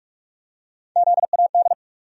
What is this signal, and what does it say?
Morse code, heard as a keyed tone, about 47 wpm, 700 Hz. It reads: ZAD
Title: USA Call Sign Postfixes - 3 Letters 45wpm